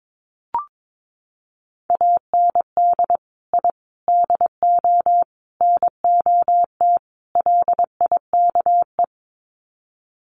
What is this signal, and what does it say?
Morse code, heard as a keyed tone, about 22 wpm, 700 Hz. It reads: AND I DO NOT LIKE
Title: Sets of 5 Words Encore2 22wpm